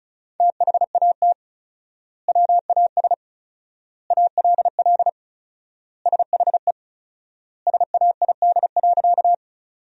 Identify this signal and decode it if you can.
Morse code, heard as a keyed tone, about 36 wpm, 700 Hz. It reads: THAT WAS ALL SHE SAID.
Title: 4x Word-Spacing - Sentences from Top 100 Words - 35wpm